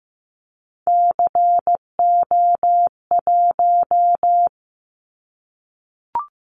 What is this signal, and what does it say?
Morse code, heard as a keyed tone, about 15 wpm, 700 Hz. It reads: CO1
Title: Callsign Prefixes for Top 100 DXCC - 15wpm